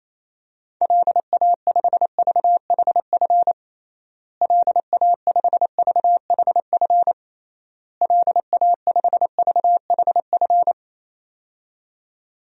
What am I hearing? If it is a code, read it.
Morse code, heard as a keyed tone, about 28 wpm, 700 Hz. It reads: LA5VHF LA5VHF LA5VHF
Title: DXCC-100 Call Signs - 6 Characters - Repeated 3x - 28wpm